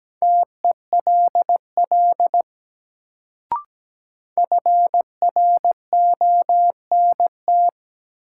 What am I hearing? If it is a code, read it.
Morse code, heard as a keyed tone, about 17 wpm, 700 Hz. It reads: TELL FRONT
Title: Sets of 2 Words - New Word Drill - 17wpm